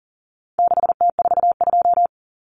Morse code, heard as a keyed tone, about 40 wpm, 700 Hz. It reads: 6T42
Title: Common Cut Numbers - 4-Digit 40wpm